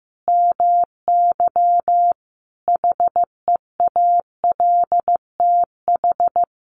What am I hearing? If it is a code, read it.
Morse code, heard as a keyed tone, about 15 wpm, 700 Hz. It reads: MY HEALTH
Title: Sentences from Top 1200 Words 15wpm